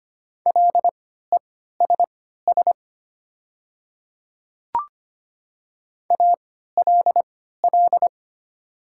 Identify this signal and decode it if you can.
Morse code, heard as a keyed tone, about 25 wpm, spaced as for 14 wpm, 690 Hz. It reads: LESS ALL
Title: 3x Character Spacing - Top 500 Words - 25wpm